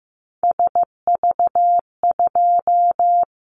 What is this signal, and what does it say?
Morse code, heard as a keyed tone, about 15 wpm, 700 Hz. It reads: SV2
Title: Callsign Prefixes for Top 50 DXCC - 15wpm